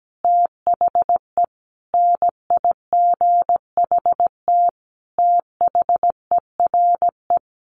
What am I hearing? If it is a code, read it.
Morse code, heard as a keyed tone, about 17 wpm, 700 Hz. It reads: THE NIGHT THERE
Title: Sentences from Top 1500 Words 17wpm